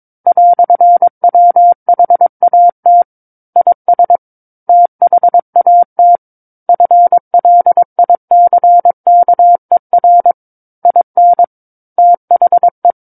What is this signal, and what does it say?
Morse code, heard as a keyed tone, about 22 wpm, 700 Hz. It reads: "WHAT IS THAT FLICKER IN THE
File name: Book One / 13-Chapter 13 - 22 WPM